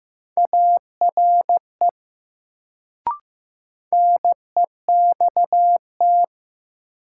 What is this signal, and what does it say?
Morse code, heard as a keyed tone, about 15 wpm, 700 Hz. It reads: ARE NEXT
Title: Sets of 2 Words - New Word Drill - 15wpm